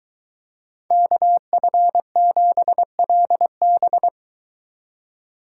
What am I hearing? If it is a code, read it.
Morse code, heard as a keyed tone, about 22 wpm, 700 Hz. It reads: KF7LB